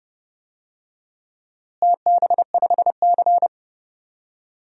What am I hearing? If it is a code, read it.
Morse code, heard as a keyed tone, about 30 wpm, 700 Hz. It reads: TB5C
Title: DXCC-100 Call Signs - 4 Characters - Repeated 2x - 30wpm